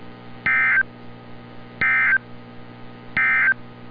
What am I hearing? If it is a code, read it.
Emergency Alert System (SAME) tones, heard as harsh buzzing bursts. It is heard from 0.3 to 3.6 s.
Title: NOAA Weather Radio - Altoona, PA